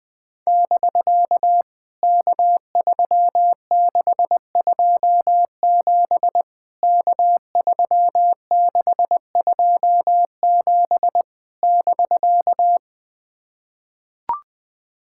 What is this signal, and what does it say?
Morse code, heard as a keyed tone, about 20 wpm, 700 Hz. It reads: <BK> K3627 K3627 <BK>
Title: POTA QSO Practice - 20wpm